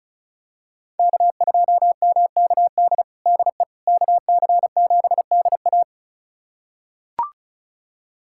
Morse code, heard as a keyed tone, about 36 wpm, 700 Hz. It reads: K2MKD DE KC7DA
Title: USA Callsign DE Callsign - 5 Characters - 35wpm